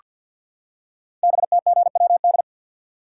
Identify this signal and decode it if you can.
Morse code, heard as a keyed tone, about 48 wpm, 690 Hz. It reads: BTGWD
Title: Code Groups - 5 Letters - 50wpm